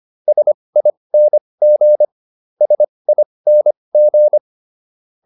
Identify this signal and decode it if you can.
Morse code, heard as a keyed tone, about 25 wpm, spaced as for 20 wpm, 600 Hz. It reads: SING SING